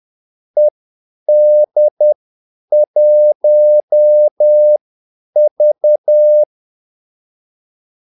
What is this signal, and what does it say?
Morse code, heard as a keyed tone, about 10 wpm, spaced as for 8 wpm, 600 Hz. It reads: ED1V
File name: Callsigns-10-8wpm